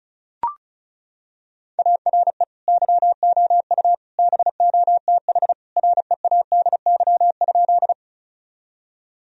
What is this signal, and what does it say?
Morse code, heard as a keyed tone, about 36 wpm, 700 Hz. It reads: ARE YOU BOTH READY?
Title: Sentences from Top 1000 Words - Review Part 1 - 35wpm